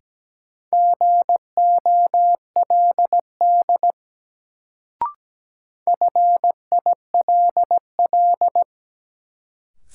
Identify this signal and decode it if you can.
Morse code, heard as a keyed tone, about 17 wpm, 700 Hz. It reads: GOLD FILL
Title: Sentences from Top 1000 Words - Review Part 3 - New Word Drill - 17wpm